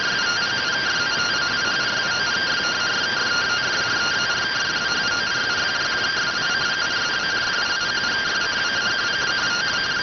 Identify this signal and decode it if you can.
Two-tone teleprinter signal, about 170 Hz shift.